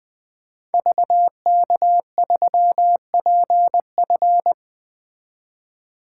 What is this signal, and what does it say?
Morse code, heard as a keyed tone, about 20 wpm, 700 Hz. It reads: VK3PF
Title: 2023 SOTA Activator Call Signs - 20wpm